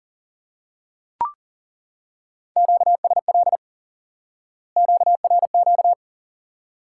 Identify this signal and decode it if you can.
Morse code, heard as a keyed tone, about 40 wpm, 700 Hz. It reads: QSL QRQ
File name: IQE Course 3 - Lesson 89 - YAGI - Two Elements - 40wpm